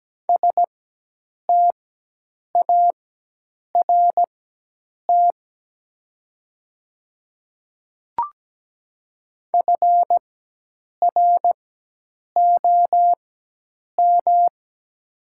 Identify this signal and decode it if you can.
Morse code, heard as a keyed tone, about 17 wpm, spaced as for 8 wpm, 700 Hz. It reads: START FROM
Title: 4x Character Spacing - Top 500 Words - 17wpm